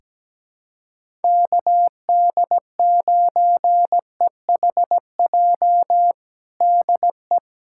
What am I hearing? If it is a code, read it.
Morse code, heard as a keyed tone, about 17 wpm, 700 Hz. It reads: KD9EHJ DE
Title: USA Callsign DE Callsign 17wpm